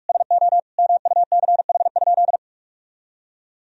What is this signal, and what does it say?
Morse code, heard as a keyed tone, about 41 wpm, 700 Hz. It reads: SO MUCH?
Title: Sentences from Top 1000 Words - Review Part 2 - 45wpm